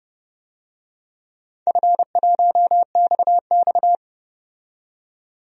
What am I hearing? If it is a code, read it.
Morse code, heard as a keyed tone, about 29 wpm, 700 Hz. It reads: F1XX
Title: DXCC-100 Call Signs - 4 Characters - 30wpm